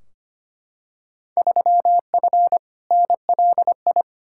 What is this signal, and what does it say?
Morse code, heard as a keyed tone, about 25 wpm, 700 Hz. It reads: 3F NLI
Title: Field Day Exchanges - 25wpm